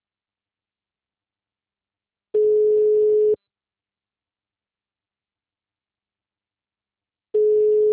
Ringback tone (Europe, 425 Hz).
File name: out- -2017- - - .191